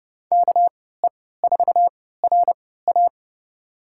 Morse code, heard as a keyed tone, about 30 wpm, spaced as for 17 wpm, 700 Hz. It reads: KE4RA
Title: USA Call Signs - 5 Characters - 3x Character Spacing - 30wpm